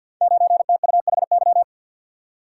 Morse code, heard as a keyed tone, about 51 wpm, 700 Hz. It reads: 9TUHY
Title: Code Groups - 5 Letter:Numbers - 50wpm